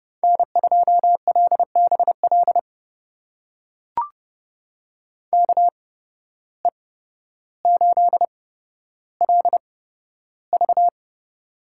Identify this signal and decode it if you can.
Morse code, heard as a keyed tone, about 30 wpm, spaced as for 13 wpm, 700 Hz. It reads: N2LBL KE8LV
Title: USA Call Signs - 5 Characters - 8x Character Spacing - 30wpm